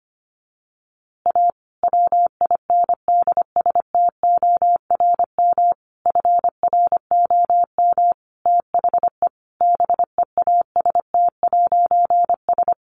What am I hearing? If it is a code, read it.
Morse code, heard as a keyed tone, about 25 wpm, 700 Hz. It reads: A WINDSTORM FROM THE BEAST'S
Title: A Sound Of Thunder 25wpm